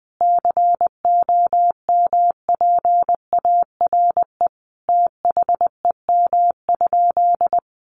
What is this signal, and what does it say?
Morse code, heard as a keyed tone, about 20 wpm, 700 Hz. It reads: COMPARE THEM?
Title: Sentences from Top 1300 Words 20wpm